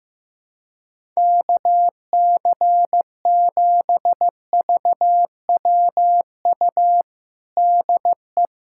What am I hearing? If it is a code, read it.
Morse code, heard as a keyed tone, about 15 wpm, 700 Hz. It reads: KC7VWU DE
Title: USA Callsign DE Callsign 15wpm